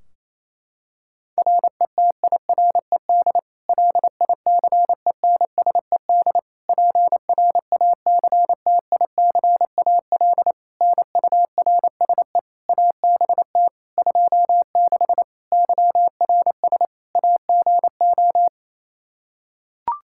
Morse code, heard as a keyed tone, about 27 wpm, 700 Hz. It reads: RETIRED LICENSED PRACTICAL NURSE ABT 26 YRS AGO
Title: Practice QSO - Exchange 3 - More Info - 28wpm